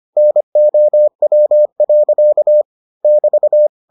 Morse code, heard as a keyed tone, about 25 wpm, 600 Hz. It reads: NOW. <BT>